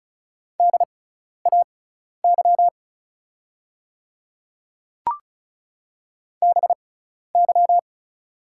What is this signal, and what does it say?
Morse code, heard as a keyed tone, about 36 wpm, spaced as for 12 wpm, 700 Hz. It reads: DAY BY
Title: 6x Character Spacing - Top 100 Words - 35wpm